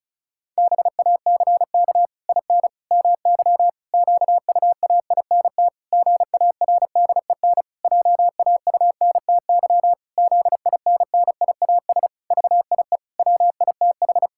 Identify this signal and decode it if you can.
Morse code, heard as a keyed tone, about 36 wpm, 700 Hz. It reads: BACK IN MY QUAINT GARDEN JAUNTY ZINNIAS VIE WITH
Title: Pangram Challenge v2 - 35wpm